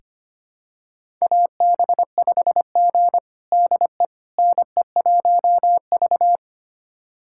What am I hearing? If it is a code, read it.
Morse code, heard as a keyed tone, about 25 wpm, 700 Hz. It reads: AB5G DE NE1V
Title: USA Callsign DE Callsign - 4 Characters - 25wpm